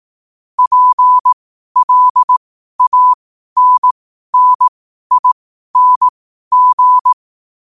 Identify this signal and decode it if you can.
Morse code, heard as a keyed tone, about 18 wpm, spaced as for 12 wpm, 1000 Hz. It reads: PLANNING
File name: qotdfiles / 12wpm